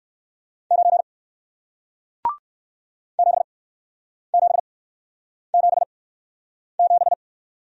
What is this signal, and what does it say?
Morse code, heard as a keyed tone, about 61 wpm, spaced as for 9 wpm, 700 Hz. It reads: 9 7777